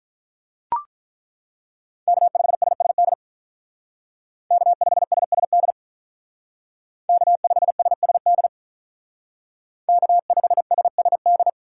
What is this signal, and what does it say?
Morse code, heard as a keyed tone, about 42 wpm, 700 Hz. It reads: K5SSD K5SSD K5SSD K5SSD
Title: Speed Racing - USA 1x3 Call Signs 35wpm